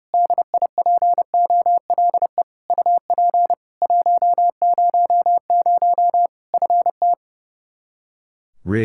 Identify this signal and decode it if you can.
Morse code, heard as a keyed tone, about 30 wpm, 700 Hz. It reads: DIPOLE UP 100 FT
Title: Practice QSO - Exchange 2 - Info - 30wpm